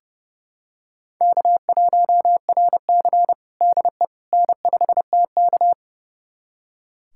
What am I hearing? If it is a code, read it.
Morse code, heard as a keyed tone, about 32 wpm, 700 Hz. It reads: K1RC DE N5TK
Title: USA Callsign DE Callsign - 4 Characters - 30wpm